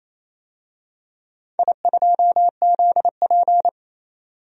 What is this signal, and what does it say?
Morse code, heard as a keyed tone, about 28 wpm, 700 Hz. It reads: I2ZP
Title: DXCC-100 Call Signs - 4 Characters - 28wpm